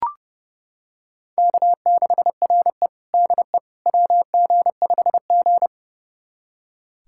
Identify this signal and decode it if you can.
Morse code, heard as a keyed tone, about 30 wpm, 700 Hz. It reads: K6RE DE WG5G
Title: USA Callsign DE Callsign - 4 Characters - 30wpm